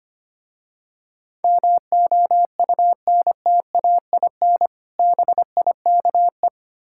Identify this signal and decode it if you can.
Morse code, heard as a keyed tone, about 25 wpm, 700 Hz. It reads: MOUNTAIN BIKE
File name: practice / 25wpm / Difficult Words with Hints - 25wpm